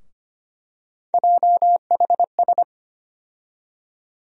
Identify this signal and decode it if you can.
Morse code, heard as a keyed tone, about 25 wpm, 700 Hz. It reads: JHS
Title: USA Call Sign Postfixes - 3 Letters 25wpm